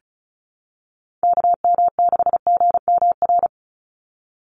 Morse code, heard as a keyed tone, about 34 wpm, 700 Hz. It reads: KM6GMR
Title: USA Call Signs 35wpm